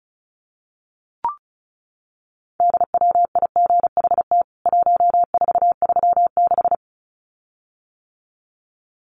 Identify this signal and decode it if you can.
Morse code, heard as a keyed tone, about 34 wpm, 700 Hz. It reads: DWIGHT 1436
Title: CWT Exchange 35wpm